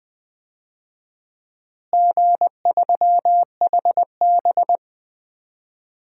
Morse code, heard as a keyed tone, about 20 wpm, 700 Hz. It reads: G3HB
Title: DXCC-100 Call Signs - 4 Characters - Repeated 3x - 20wpm